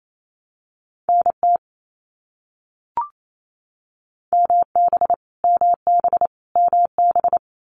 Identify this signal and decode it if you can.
Morse code, heard as a keyed tone, about 28 wpm, 700 Hz. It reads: NT MB MB MB
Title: Canadian Provinces and Territories - Repeated 3x - 28wpm